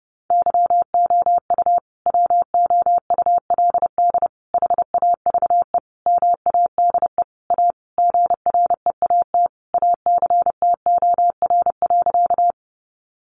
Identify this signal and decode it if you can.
Morse code, heard as a keyed tone, about 30 wpm, 700 Hz. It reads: YOU WOULD HAVE MADE A GREAT ACTOR.
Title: Sentences from Top 1500 Words 30wpm